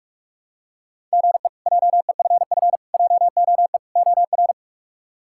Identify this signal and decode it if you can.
Morse code, heard as a keyed tone, about 46 wpm, 700 Hz. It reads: GE JEFF JOE OR
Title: K1USN SST QSO Practice - 45wpm